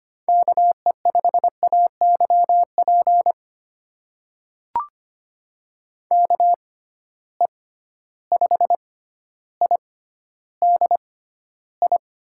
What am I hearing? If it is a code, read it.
Morse code, heard as a keyed tone, about 25 wpm, spaced as for 13 wpm, 700 Hz. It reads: KE5AYP KE5IDI
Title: USA Call Signs - 6 Characters - 6x Character Spacing - 25wpm